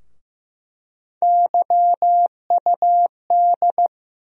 Morse code, heard as a keyed tone, about 15 wpm, 700 Hz. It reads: YUD
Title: USA Call Sign Postfixes - 3 Letters 15wpm